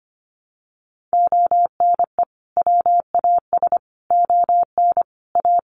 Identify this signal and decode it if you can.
Morse code, heard as a keyed tone, about 25 wpm, 700 Hz. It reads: ONE WAS ON A
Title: Sentences from Top 300 Words 25wpm